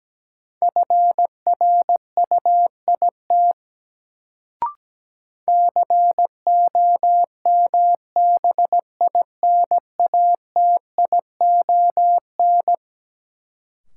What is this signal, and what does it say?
Morse code, heard as a keyed tone, about 17 wpm, 700 Hz. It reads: FRUIT COMBINATION
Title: Sentences from Top 1300 Words - New Word Drill Alt - 17wpm